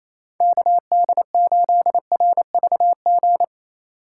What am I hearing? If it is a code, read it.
Morse code, heard as a keyed tone, about 28 wpm, 700 Hz. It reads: KD8RVG
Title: USA Call Signs - 6 Characters - 2x Character Spacing - 28wpm